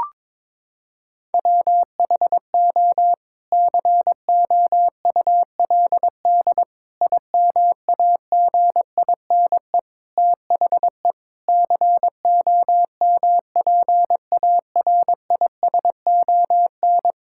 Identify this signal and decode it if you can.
Morse code, heard as a keyed tone, about 22 wpm, 700 Hz. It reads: WHO COULD IMAGINE THE COMPARISON
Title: Sentences from Top 1600 Words 22wpm